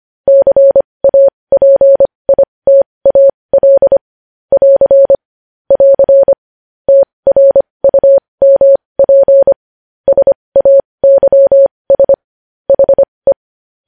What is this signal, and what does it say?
Morse code, heard as a keyed tone, about 25 wpm, spaced as for 20 wpm, 550 Hz. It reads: CAPITAL <AR> <AR> TRUMP SAYS HE
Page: News Headlines in Morse Code at 20 WPM